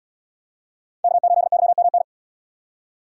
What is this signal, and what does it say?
Morse code, heard as a keyed tone, about 77 wpm, 700 Hz. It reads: K8YMN